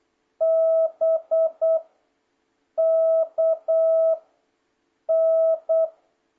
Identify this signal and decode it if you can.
Morse code, heard as a keyed tone, about 8 wpm, 650 Hz. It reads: B K N